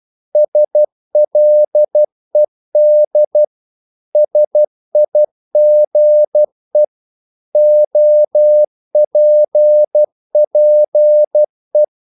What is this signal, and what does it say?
Morse code, heard as a keyed tone, about 12 wpm, 600 Hz. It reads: SLED SIGE OPPE
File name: Korte ord DK 12wpm